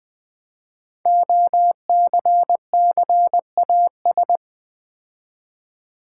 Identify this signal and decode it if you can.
Morse code, heard as a keyed tone, about 20 wpm, 700 Hz. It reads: OCCAS
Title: Build A Word - 5 Letters At A Time - 12-Letter Words - 20wpm